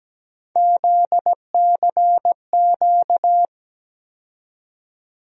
Morse code, heard as a keyed tone, about 17 wpm, 700 Hz. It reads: ZCQ
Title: USA Call Sign Postfixes - 3 Letters 17wpm